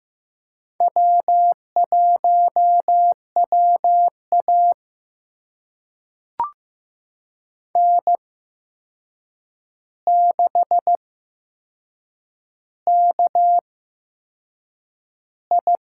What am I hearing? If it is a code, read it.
Morse code, heard as a keyed tone, about 15 wpm, spaced as for 6 wpm, 700 Hz. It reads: W1WA N6KI
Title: USA Call Signs - 4 Characters - 8x Character Spacing - 15wpm